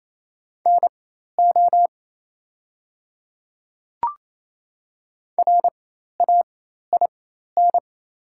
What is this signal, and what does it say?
Morse code, heard as a keyed tone, about 28 wpm, spaced as for 13 wpm, 700 Hz. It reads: NO RAIN
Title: 4x Character Spacing - Top 300 Words - 28wpm